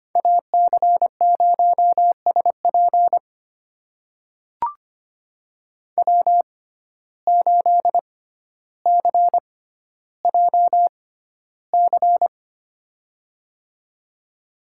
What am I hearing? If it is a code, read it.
Morse code, heard as a keyed tone, about 25 wpm, spaced as for 13 wpm, 700 Hz. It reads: AC0SP W8CJC
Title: USA Call Signs - 5 Characters - 6x Character Spacing - 25wpm